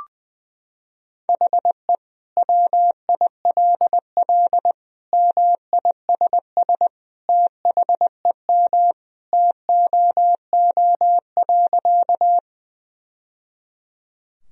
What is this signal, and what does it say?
Morse code, heard as a keyed tone, about 20 wpm, 700 Hz. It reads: HE WILL MISS THEM TOO.
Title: Sentences from Top 1000 Words - Review Part 3 - 20wpm